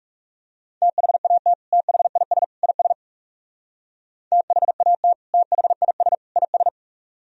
Morse code, heard as a keyed tone, about 35 wpm, 700 Hz. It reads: THAT THIS IS THAT THIS IS
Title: Speed-Racing - Sets of 3 Words 30wpm